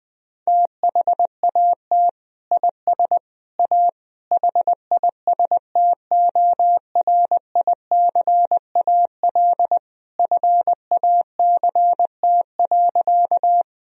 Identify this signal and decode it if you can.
Morse code, heard as a keyed tone, about 20 wpm, 700 Hz. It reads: THAT IS A HISTORICAL FACT.
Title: Sentences from Top 1000 Words - Review Part 4 - 20wpm